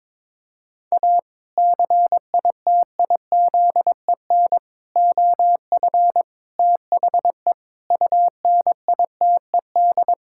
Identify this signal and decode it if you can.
Morse code, heard as a keyed tone, about 22 wpm, 700 Hz. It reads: A CITIZEN OF THE UNITED
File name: practice / 22wpm / Sentences with 3-Word Phrases - 22wpm